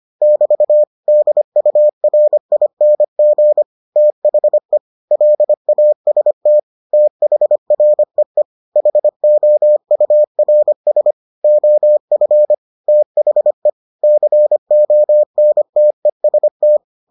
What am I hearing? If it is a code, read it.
Morse code, heard as a keyed tone, about 25 wpm, 600 Hz. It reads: <BT> DURING THE LAST THREE HOURS OF THE CONTEST